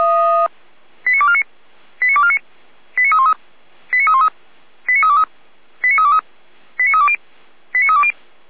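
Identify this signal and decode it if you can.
Five-tone selective call (ZVEI), heard as quick runs of five tones, tones of about 70 ms. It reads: 89249 89249 89313 89313 89323 89323 89320 89320
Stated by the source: ZVEI five-tone selective call received on SDR, from audio.f85360.feuerwehr-orlamuende.org (Feuerwehr Orlamünde, fire brigade).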